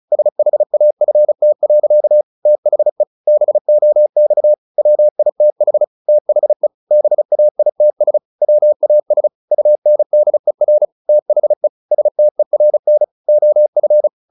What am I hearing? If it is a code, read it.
Morse code, heard as a keyed tone, about 34 wpm, 600 Hz. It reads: SHAFT. THE BOX WITH THE BAITS WAS UNDER THE STERN OF
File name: the old man and the sea / the old man and the sea 1 35wpm